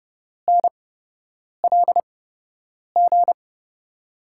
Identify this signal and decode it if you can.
Morse code, heard as a keyed tone, about 30 wpm, spaced as for 8 wpm, 700 Hz. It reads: NLG
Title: USA Call Signs - 6 Characters - 8x Character Spacing - 30wpm